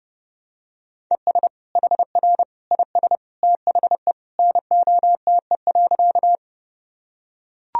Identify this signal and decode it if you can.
Morse code, heard as a keyed tone, about 30 wpm, 700 Hz. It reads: ES HR IS THE NOTE.
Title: Sentences from Top 500 Words and Abbreviations 30wpm